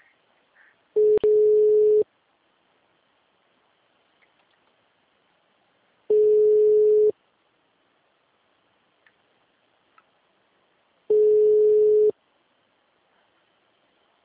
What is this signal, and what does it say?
Ringback tone (Europe, 425 Hz).
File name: out- -2005- - - .55